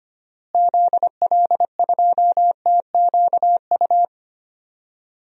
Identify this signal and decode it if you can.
Morse code, heard as a keyed tone, about 25 wpm, 700 Hz. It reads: ZL2TQU
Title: DXCC-100 Call Signs - 6 Characters - Repeated 2x - 25wpm